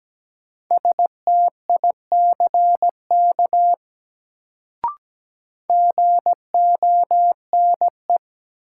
Morse code, heard as a keyed tone, about 17 wpm, 700 Hz. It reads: STICK GONE
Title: Sentences from Top 1200 Words - New Word Drill - 17wpm